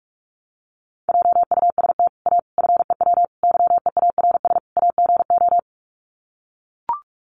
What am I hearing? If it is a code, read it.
Morse code, heard as a keyed tone, about 44 wpm, 700 Hz. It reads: JUST A FEW YEARS AGO
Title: Sets of 5 Words Encore 45wpm